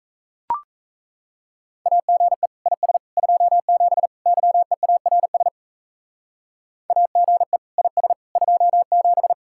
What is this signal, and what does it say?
Morse code, heard as a keyed tone, about 34 wpm, 700 Hz. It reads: AGE IS 27 YEARS AGE IS 27
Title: Speed-Racing Common QSO Phrases Encore - 28wpm